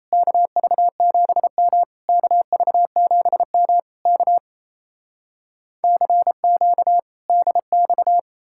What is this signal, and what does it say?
Morse code, heard as a keyed tone, about 37 wpm, 700 Hz. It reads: KV7M KV7M K CQ DX